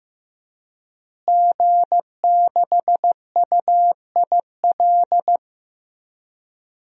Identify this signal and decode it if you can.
Morse code, heard as a keyed tone, about 15 wpm, 700 Hz. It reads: G6UIL
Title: DXCC-100 Call Signs - 5 Characters - Repeated 3x - 15wpm